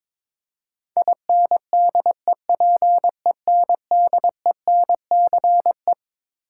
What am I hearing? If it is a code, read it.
Morse code, heard as a keyed tone, about 22 wpm, 700 Hz. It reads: INDEPENDENCE
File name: Twelve Letter Words - 22wpm